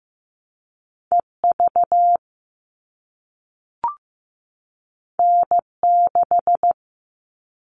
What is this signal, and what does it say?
Morse code, heard as a keyed tone, about 15 wpm, 700 Hz. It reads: EV N6
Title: All Cut Numbers - 2-Digit 15wpm